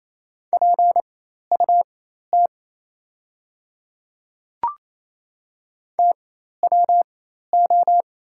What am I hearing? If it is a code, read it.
Morse code, heard as a keyed tone, about 28 wpm, spaced as for 13 wpm, 700 Hz. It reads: PUT TWO
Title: 4x Character Spacing - Top 200 Words - 28wpm